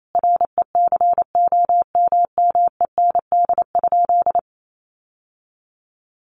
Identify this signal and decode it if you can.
Morse code, heard as a keyed tone, about 28 wpm, 700 Hz. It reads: RECOMMEND?
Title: Sentences from Top 1200 Words 28wpm